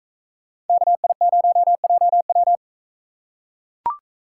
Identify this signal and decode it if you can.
Morse code, heard as a keyed tone, about 41 wpm, 700 Hz. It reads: KI0JW